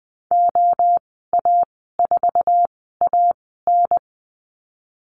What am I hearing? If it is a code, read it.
Morse code, heard as a keyed tone, about 20 wpm, spaced as for 15 wpm, 700 Hz. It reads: OA4AN
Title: 076 Lesson 13g - Letter L - Call Signs 20wpm